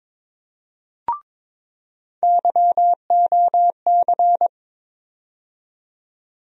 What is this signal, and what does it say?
Morse code, heard as a keyed tone, about 21 wpm, 700 Hz. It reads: YOC
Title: USA Call Sign Postfixes - 3 Letters 22wpm